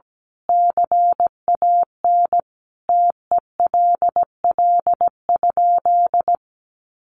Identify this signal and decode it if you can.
Morse code, heard as a keyed tone, about 17 wpm, 700 Hz. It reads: CAN TELL?
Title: A Sound Of Thunder 17wpm